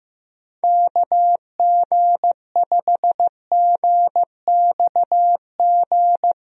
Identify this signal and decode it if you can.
Morse code, heard as a keyed tone, about 15 wpm, 700 Hz. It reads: KG5GXG
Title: USA Call Signs - 6 Characters - Repeated 2x - 15wpm